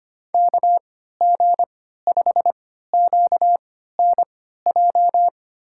Morse code, heard as a keyed tone, about 25 wpm, spaced as for 14 wpm, 700 Hz. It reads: KG5QNJ
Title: USA Call Signs - 3x Character Spacing - 25wpm